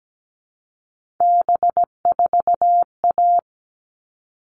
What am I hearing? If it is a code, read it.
Morse code, heard as a keyed tone, about 17 wpm, 700 Hz. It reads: B4A
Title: All Cut Numbers - 3-Digit 17wpm